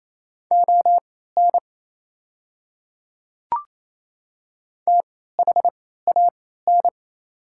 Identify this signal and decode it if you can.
Morse code, heard as a keyed tone, about 28 wpm, spaced as for 16 wpm, 690 Hz. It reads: ON THAN
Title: 3x Character Spacing - Top 100 Words - 28wpm